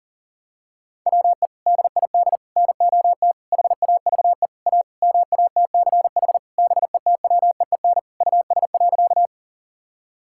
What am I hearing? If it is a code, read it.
Morse code, heard as a keyed tone, about 40 wpm, 700 Hz. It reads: WE DID NOT HAVE A MATCH BETWEEN US.
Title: Sentences from Top 1600 Words 40wpm